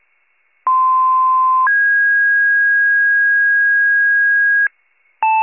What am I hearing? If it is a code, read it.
Two-tone page, heard as a one-second tone, then a three-second tone, about 1040 then 1650 Hz. It is heard from 0.7 to 4.7 s.
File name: 95 RESCUE DUTY